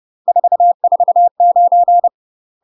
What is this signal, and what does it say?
Morse code, heard as a keyed tone, about 30 wpm, 700 Hz. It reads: 449
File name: UK MASTER QSOs TRAINING / 30WPM / UK MASTER QSOs 0021